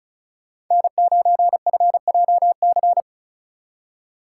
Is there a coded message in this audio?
Yes: Morse code from 0.7 to 3.0 s, heard as a keyed tone, about 47 wpm, 700 Hz. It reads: N9FJC